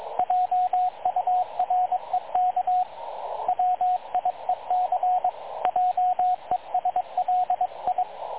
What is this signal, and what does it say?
Morse code, heard as a keyed tone, about 22 wpm, 730 Hz, with radio noise. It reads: JUREK WIEC JESLI
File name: dlazaawansowanych / nagraniazpasma / websdr recording 3567.0kHz